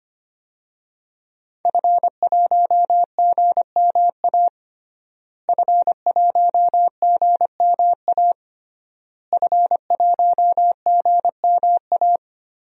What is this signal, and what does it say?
Morse code, heard as a keyed tone, about 25 wpm, 700 Hz. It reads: F1GMA F1GMA F1GMA
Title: DXCC-100 Call Signs - 5 Characters - Repeated 3x - 25wpm